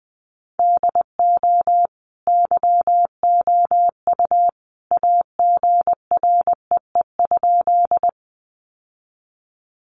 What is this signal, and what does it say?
Morse code, heard as a keyed tone, about 20 wpm, 700 Hz. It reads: DO YOU AGREE?
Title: Sentences from Top 1100 Words 20wpm